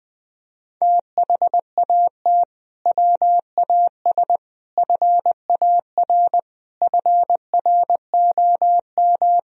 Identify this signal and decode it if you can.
Morse code, heard as a keyed tone, about 20 wpm, 700 Hz. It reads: THAT WAS FAR FROM
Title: Sentences from Top 1000 Words - Review Part 3 - 20wpm